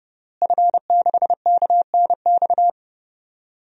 Morse code, heard as a keyed tone, about 30 wpm, 700 Hz. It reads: F6KNX
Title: DXCC-100 Call Signs - 5 Characters - Repeated 3x - 30wpm